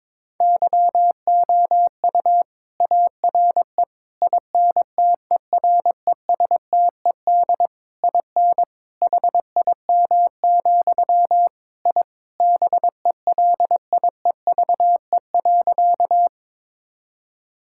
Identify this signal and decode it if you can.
Morse code, heard as a keyed tone, about 22 wpm, 700 Hz. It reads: YOU ARE INTERESTED IN HIM, I BELIEVE.
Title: Sentences from Top 1000 Words - Review Part 6 - 22wpm